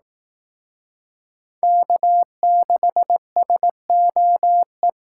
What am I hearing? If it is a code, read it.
Morse code, heard as a keyed tone, about 18 wpm, 700 Hz. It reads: K6SOE
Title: Speed Racing - USA 1x3 Call Signs 15wpm